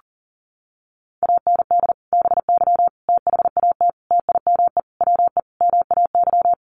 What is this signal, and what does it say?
Morse code, heard as a keyed tone, about 37 wpm, 700 Hz. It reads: AND BY THAT TIME WE MAY
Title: Sentences from Top 1100 Words 40wpm